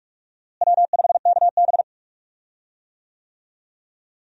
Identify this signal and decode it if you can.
Morse code, heard as a keyed tone, about 47 wpm, 700 Hz. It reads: W5KB
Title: USA Call Signs - 4 Characters - Repeated 2x - 45wpm